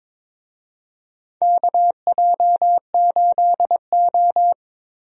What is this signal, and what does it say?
Morse code, heard as a keyed tone, about 22 wpm, 700 Hz. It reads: KJ8O
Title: USA Call Signs - 4 Characters - Repeated 2x - 22wpm